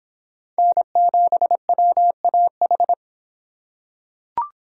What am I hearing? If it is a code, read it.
Morse code, heard as a keyed tone, about 26 wpm, 700 Hz. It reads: N7WAH